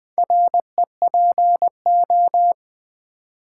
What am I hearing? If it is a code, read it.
Morse code, heard as a keyed tone, about 20 wpm, 700 Hz. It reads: REPO
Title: Build A Word - 2 Letters At A Time - 8-Letter Words - 20wpm